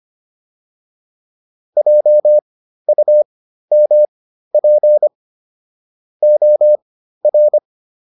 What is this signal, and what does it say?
Morse code, heard as a keyed tone, about 25 wpm, spaced as for 13 wpm, 600 Hz. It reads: JUMP OR